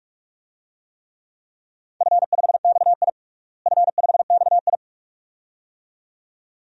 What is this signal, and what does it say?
Morse code, heard as a keyed tone, about 46 wpm, 700 Hz. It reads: F5XI F5XI
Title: DXCC-100 Call Signs - 4 Characters - Repeated 2x - 45wpm